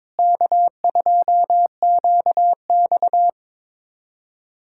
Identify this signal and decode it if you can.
Morse code, heard as a keyed tone, about 22 wpm, 700 Hz. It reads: K2QX
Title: USA Callsign DE Callsign - 4 Characters - 22wpm